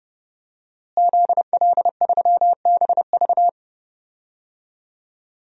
Morse code, heard as a keyed tone, about 30 wpm, 700 Hz. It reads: ZL3BV
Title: DXCC-100 Call Signs - 5 Characters - Repeated 3x - 30wpm